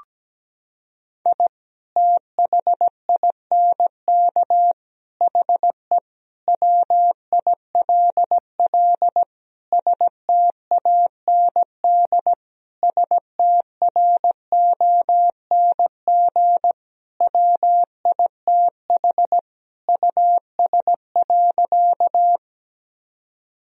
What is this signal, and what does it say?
Morse code, heard as a keyed tone, about 17 wpm, 700 Hz. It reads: I THINK HE WILL STAND STRONG WITH US.
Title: Sentences from Top 1000 Words - Review Part 3 - 17wpm